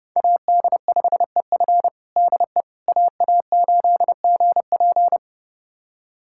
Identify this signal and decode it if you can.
Morse code, heard as a keyed tone, about 29 wpm, 700 Hz. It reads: AD5EF DE AA8GP
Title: USA Callsign DE Callsign - 5 Characters - 30wpm